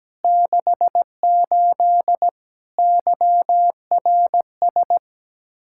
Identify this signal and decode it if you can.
Morse code, heard as a keyed tone, about 17 wpm, 700 Hz. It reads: 68 YRS
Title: Practice QSO - Exchange 3 - More Info - 17wpm